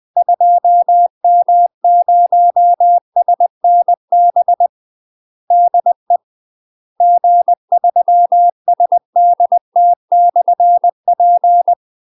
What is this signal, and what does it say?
Morse code, heard as a keyed tone, about 20 wpm, 700 Hz. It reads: 2M0SNB DE G3SDT/P
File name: UK MASTER QSOs TRAINING / 20WPM / UK MASTER QSOs 0017